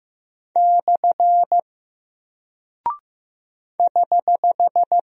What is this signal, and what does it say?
Morse code, heard as a keyed tone, about 15 wpm, 700 Hz. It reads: / <HH>
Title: Single Character (Letter:Number:Punctuation:Prosign) Alt 15wpm